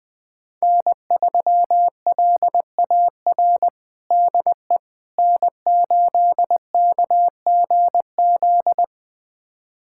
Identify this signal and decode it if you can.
Morse code, heard as a keyed tone, about 20 wpm, 700 Hz. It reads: N3LAR DE N8KGZ
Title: USA Callsign DE Callsign - 5 Characters - 20wpm